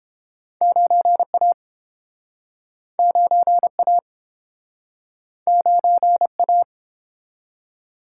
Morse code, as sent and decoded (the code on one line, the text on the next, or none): ----. .- / ----. .- / ----. .-
9A 9A 9A